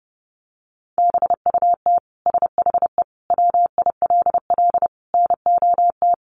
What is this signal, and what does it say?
Morse code, heard as a keyed tone, about 30 wpm, 700 Hz. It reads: BUT SHE WILL NOT
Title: Sentences from Top 1200 Words 30wpm